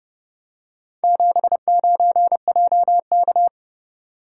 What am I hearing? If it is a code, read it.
Morse code, heard as a keyed tone, about 31 wpm, 700 Hz. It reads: 79JK